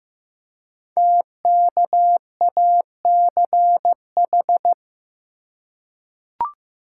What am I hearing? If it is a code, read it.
Morse code, heard as a keyed tone, about 15 wpm, 700 Hz. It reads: TKACH
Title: Code Groups - 5 Letters - 15wpm